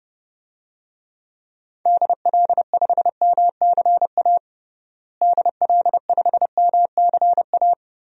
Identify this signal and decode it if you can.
Morse code, heard as a keyed tone, about 32 wpm, 700 Hz. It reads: DL5MCA DL5MCA
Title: DXCC-100 Call Signs - 6 Characters - Repeated 2x - 30wpm